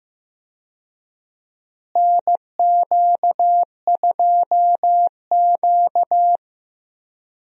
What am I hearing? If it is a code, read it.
Morse code, heard as a keyed tone, about 15 wpm, 700 Hz. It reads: NQ2Q
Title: USA Call Signs - 4 Characters - Repeated 3x - 15wpm